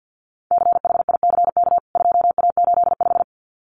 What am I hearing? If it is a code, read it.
Morse code, heard as a keyed tone, about 50 wpm, 700 Hz. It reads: CHICK 2A85
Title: CWT Exchange 50wpm